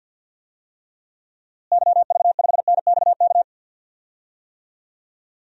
Morse code, heard as a keyed tone, about 50 wpm, 700 Hz. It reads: YV5NXK
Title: DXCC-100 Call Signs - 6 Characters - Repeated 2x - 50wpm